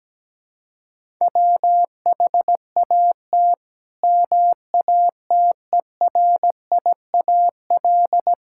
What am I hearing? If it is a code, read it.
Morse code, heard as a keyed tone, about 17 wpm, 700 Hz. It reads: WHAT MATERIAL
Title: Sentences from Top 1000 Words - Review Part 5 - 17wpm